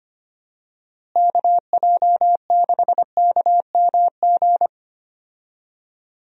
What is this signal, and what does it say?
Morse code, heard as a keyed tone, about 25 wpm, 700 Hz. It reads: KJ6KMG
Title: USA Call Signs - 6 Characters - Repeated 2x - 25wpm